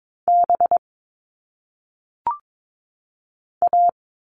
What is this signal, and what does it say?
Morse code, heard as a keyed tone, about 22 wpm, 700 Hz. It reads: B A
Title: All Cut Numbers - 1-Digit 22wpm